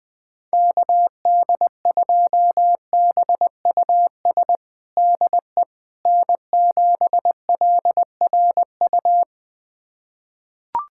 Morse code, heard as a keyed tone, about 20 wpm, 700 Hz. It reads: KD2BUS DE N7LRU
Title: USA Callsign DE Callsign 20wpm